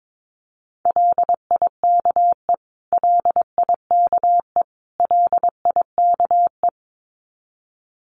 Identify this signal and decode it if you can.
Morse code, heard as a keyed tone, about 22 wpm, 700 Hz. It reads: LIKE LIKE LIKE
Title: Top 500 Words Repeated 22wpm